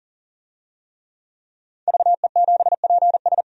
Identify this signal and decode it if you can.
Morse code, heard as a keyed tone, about 40 wpm, 700 Hz. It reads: VE7PS
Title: DXCC-100 Call Signs - 5 Characters - 40wpm